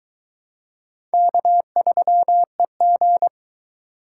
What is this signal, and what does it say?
Morse code, heard as a keyed tone, about 22 wpm, 700 Hz. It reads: K3EG